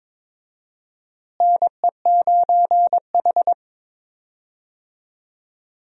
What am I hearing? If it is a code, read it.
Morse code, heard as a keyed tone, about 22 wpm, 700 Hz. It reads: NE9H
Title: USA Call Signs - 4 Characters - Repeated 3x - 22wpm